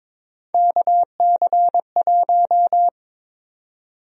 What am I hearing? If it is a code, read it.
Morse code, heard as a keyed tone, about 22 wpm, 700 Hz. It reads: KC1
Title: USA Call Sign Prefixes - 22wpm